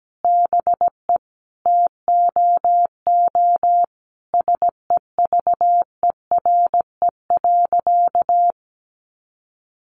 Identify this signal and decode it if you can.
Morse code, heard as a keyed tone, about 17 wpm, 700 Hz. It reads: BE TOO SEVERE.
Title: Sentences from Top 1400 Words 17wpm